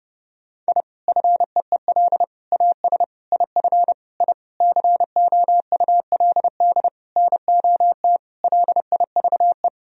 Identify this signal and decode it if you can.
Morse code, heard as a keyed tone, about 32 wpm, 700 Hz. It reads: I FEEL AS IF I COULD NOT LIVE
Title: Sentences from Top 1000 Words - Review Part 3 - 30wpm